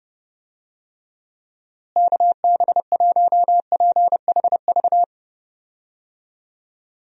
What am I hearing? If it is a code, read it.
Morse code, heard as a keyed tone, about 30 wpm, 700 Hz. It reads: KB1PHV
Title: USA Call Signs - 6 Characters - Repeated 2x - 30wpm